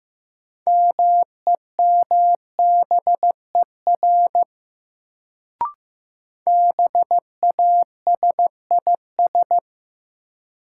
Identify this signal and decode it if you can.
Morse code, heard as a keyed tone, about 15 wpm, 700 Hz. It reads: MEMBER BASIS
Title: Sets of 3 Words - New Word Drill - 15wpm